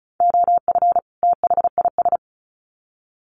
Morse code, heard as a keyed tone, about 34 wpm, 700 Hz. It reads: OF THIS
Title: Sets of 2 Words 35wpm